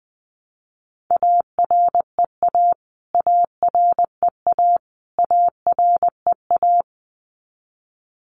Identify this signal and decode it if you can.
Morse code, heard as a keyed tone, about 20 wpm, 700 Hz. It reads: AREA AREA AREA
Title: Top 500 Words Repeated 20wpm